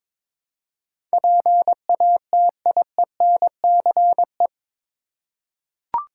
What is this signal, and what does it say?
Morse code, heard as a keyed tone, about 22 wpm, 700 Hz. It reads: PATIENCE
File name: practice / 22wpm / Eight Letter Words - 22wpm